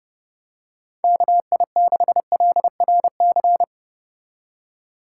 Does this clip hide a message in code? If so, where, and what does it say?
Yes: Morse code from 1.0 to 3.6 s, heard as a keyed tone, about 31 wpm, 700 Hz. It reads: KI6LRC